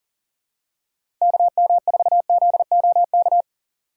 Morse code, heard as a keyed tone, about 39 wpm, 700 Hz. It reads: KM4ZOK
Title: USA Call Signs - 6 Characters - 2x Character Spacing - 40wpm